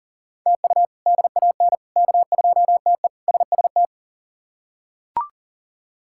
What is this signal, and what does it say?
Morse code, heard as a keyed tone, about 37 wpm, 700 Hz. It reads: TU DAN K2TE SST
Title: K1USN SST QSO Practice - 40wpm